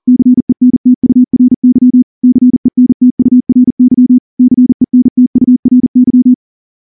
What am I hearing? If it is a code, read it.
Morse code, heard as a keyed tone, about 38 wpm, 250 Hz. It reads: CENTURY CENTURY CENTURY
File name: morse-training-40WPM